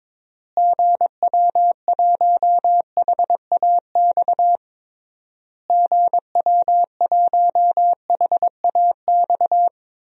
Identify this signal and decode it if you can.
Morse code, heard as a keyed tone, about 22 wpm, 700 Hz. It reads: GW1HAX GW1HAX
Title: DXCC-100 Call Signs - 6 Characters - Repeated 3x - 22wpm